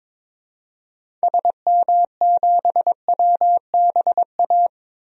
Morse code, heard as a keyed tone, about 22 wpm, 700 Hz. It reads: SM7WBA
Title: DXCC-100 Call Signs - 6 Characters - 22wpm